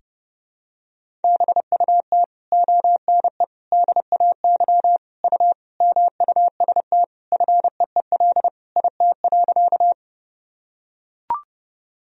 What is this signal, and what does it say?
Morse code, heard as a keyed tone, about 30 wpm, 700 Hz. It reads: BUT ONE DAY U MUST FEEL IT.
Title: Sentences from Top 400 Words and Abbreviations 30wpm